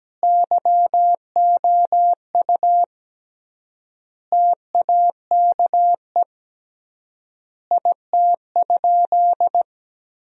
Morse code, heard as a keyed tone, about 17 wpm, 700 Hz. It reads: YOU TAKE IT?
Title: 3x Word-Spacing - Sentences from Top 200 Words - 17wpm